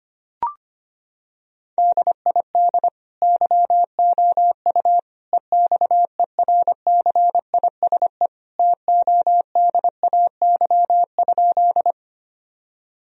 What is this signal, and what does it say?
Morse code, heard as a keyed tone, about 25 wpm, 690 Hz. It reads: DID YOU EXERCISE TODAY?
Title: Sentences from Top 1000 Words - Review Part 4 - 25wpm